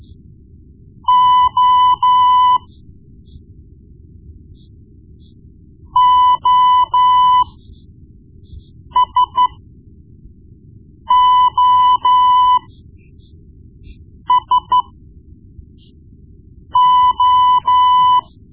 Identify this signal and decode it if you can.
Morse code, heard as a keyed tone, about 13 wpm, spaced as for 4 wpm, 1000 Hz. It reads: O OSOSO
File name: Morse Lessons / Morse Lessons / Tape-1B